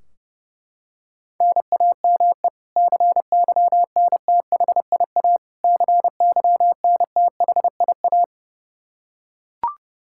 Morse code, heard as a keyed tone, about 32 wpm, 700 Hz. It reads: NAME CYNTHIA CYNTHIA
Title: K1USN SST QSO Practice 30wpm